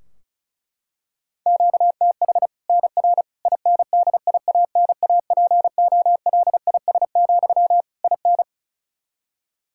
Morse code, heard as a keyed tone, about 34 wpm, 700 Hz. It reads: QTH NR INDIANAPOLIS, IN
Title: Practice QSO - Exchange 1 - Beginning - 35wpm